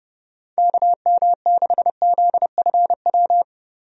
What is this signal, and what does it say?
Morse code, heard as a keyed tone, about 30 wpm, 700 Hz. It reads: KM6ZFW
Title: USA Call Signs - 6 Characters - 30wpm